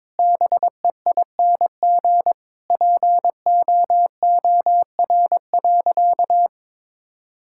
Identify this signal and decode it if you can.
Morse code, heard as a keyed tone, about 22 wpm, 700 Hz. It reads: BEING POOR.
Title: Sentences from Top 1000 Words - Review Part 6 - 22wpm